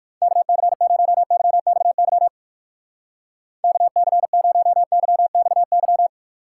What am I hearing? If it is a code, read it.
Morse code, heard as a keyed tone, about 56 wpm, 700 Hz. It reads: KC0YXY KC0YXY